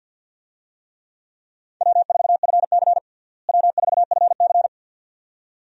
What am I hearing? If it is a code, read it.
Morse code, heard as a keyed tone, about 51 wpm, 700 Hz. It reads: W4FC W4FC
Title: USA Call Signs - 4 Characters - Repeated 2x - 50wpm